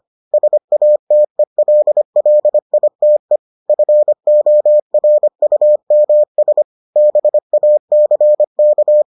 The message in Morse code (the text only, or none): SATELLITE FORUMS BACK